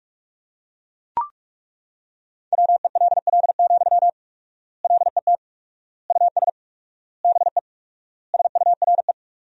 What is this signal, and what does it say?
Morse code, heard as a keyed tone, about 41 wpm, 700 Hz. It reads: WELL, LET US BE SURE
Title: 4x Word-Spacing - Sentences from Top 300 Words - 45wpm